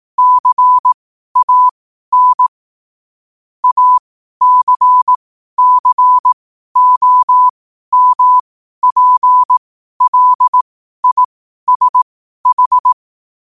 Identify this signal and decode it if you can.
Morse code, heard as a keyed tone, about 18 wpm, spaced as for 12 wpm, 1000 Hz. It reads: CAN ACCOMPLISH
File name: qotdfiles / 12wpm